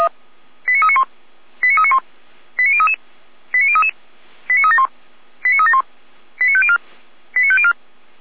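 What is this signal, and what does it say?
Five-tone selective call (ZVEI), heard as quick runs of five tones, tones of about 80 ms. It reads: 89391 89391 89033 89033 89371 89371 89504 89504
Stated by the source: Five-tone ZVEI selective call received on an SDR, channel 455.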